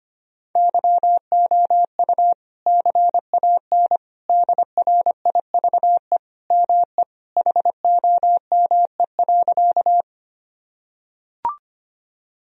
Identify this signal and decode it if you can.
Morse code, heard as a keyed tone, about 25 wpm, 700 Hz. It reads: YOU CAN DRIVE ME HOME.
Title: Sentences from Top 1000 Words - Review Part 2 - 25wpm